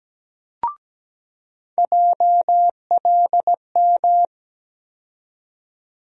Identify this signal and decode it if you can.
Morse code, heard as a keyed tone, about 17 wpm, 700 Hz. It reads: JLM
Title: USA Call Sign Postfixes - 3 Letters 17wpm